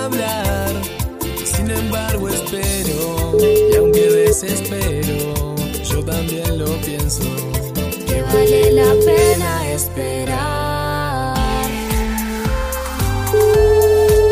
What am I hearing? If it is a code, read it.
Ringback tone (Europe, 425 Hz), with noise.